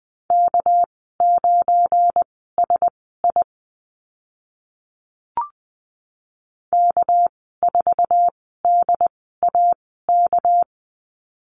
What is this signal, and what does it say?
Morse code, heard as a keyed tone, about 20 wpm, spaced as for 14 wpm, 700 Hz. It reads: K9SI K4DAK
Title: 213 Lesson 34g - Number 8 - Call Signs 20wpm